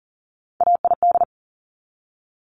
Morse code, heard as a keyed tone, about 38 wpm, 690 Hz. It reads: AID
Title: Top Three Letter Combinations in Words 40wpm